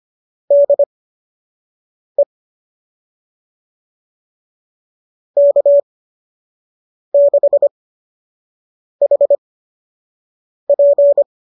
Morse code, heard as a keyed tone, about 25 wpm, spaced as for 6 wpm, 570 Hz. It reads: DE K6HP
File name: qso003 06